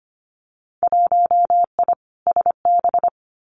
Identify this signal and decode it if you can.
Morse code, heard as a keyed tone, about 25 wpm, 700 Hz. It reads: 1I SB
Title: Winter Field Day Exchanges 25wpm